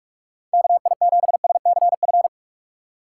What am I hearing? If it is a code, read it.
Morse code, heard as a keyed tone, about 45 wpm, 700 Hz. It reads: KI7SCF